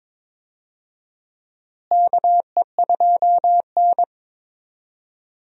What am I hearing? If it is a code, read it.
Morse code, heard as a keyed tone, about 22 wpm, 700 Hz. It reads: KE2N
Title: USA Call Signs - 4 Characters - Repeated 3x - 22wpm